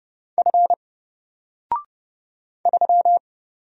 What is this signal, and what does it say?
Morse code, heard as a keyed tone, about 30 wpm, 700 Hz. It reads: F 3
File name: Single-Letter-Number-30wpm